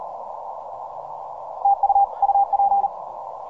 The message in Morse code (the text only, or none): TU UP